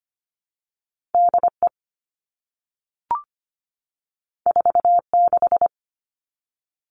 Morse code, heard as a keyed tone, about 25 wpm, 700 Hz. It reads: DE 46
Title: All Cut Numbers - 2-Digit 25wpm